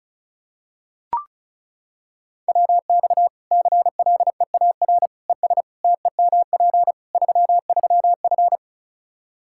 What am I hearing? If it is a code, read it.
Morse code, heard as a keyed tone, about 34 wpm, 700 Hz. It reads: WX CLEAR ES TEMP 33F
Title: Practice QSO - Exchange 2 - Info - 35wpm